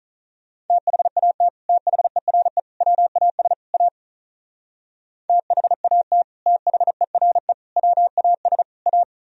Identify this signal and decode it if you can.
Morse code, heard as a keyed tone, about 43 wpm, 700 Hz. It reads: THAT THERE WAS A THAT THERE WAS A
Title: Speed-Racing - Sets of 4 Words 35wpm